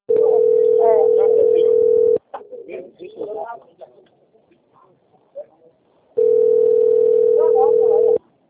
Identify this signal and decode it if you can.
Ringback tone (North America), 440 + 480 Hz, with speech.
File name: out- - - - - .19122